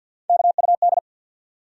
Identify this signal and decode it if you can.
Morse code, heard as a keyed tone, about 53 wpm, 700 Hz. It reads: KUD
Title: USA Call Sign Postfixes - 3 Letters 50wpm